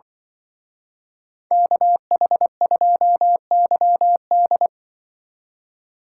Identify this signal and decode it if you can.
Morse code, heard as a keyed tone, about 25 wpm, 700 Hz. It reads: KH2YD
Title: Speed Racing - USA 2x2 Call Signs 20wpm